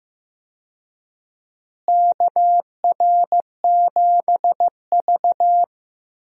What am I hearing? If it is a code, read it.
Morse code, heard as a keyed tone, about 16 wpm, 700 Hz. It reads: KR7V
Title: USA Call Signs - 4 Characters - Repeated 2x - 15wpm